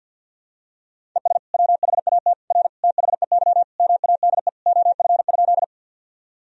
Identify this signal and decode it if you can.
Morse code, heard as a keyed tone, about 52 wpm, 700 Hz. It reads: ES WHAT R THEY MADE OF?
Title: Sentences from Top 200 Words and Abbreviations 50wpm